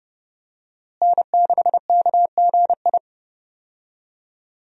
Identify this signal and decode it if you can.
Morse code, heard as a keyed tone, about 32 wpm, 700 Hz. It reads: N6KGI
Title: USA Call Signs - 5 Characters - Repeated 2x - 30wpm